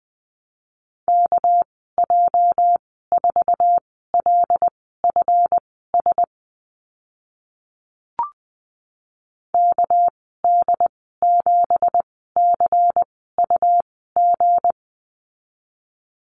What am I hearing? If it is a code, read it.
Morse code, heard as a keyed tone, about 20 wpm, spaced as for 14 wpm, 700 Hz. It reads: KJ4LFS KD7CUG
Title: 231 Lesson 37g - Letter x - Call Signs 20wpm